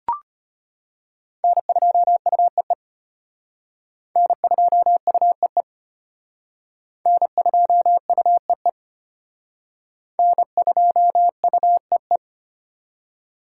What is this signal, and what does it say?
Morse code, heard as a keyed tone, about 31 wpm, 700 Hz. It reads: N2UEE N2UEE N2UEE N2UEE
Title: Speed Racing - USA 1x3 Call Signs 25wpm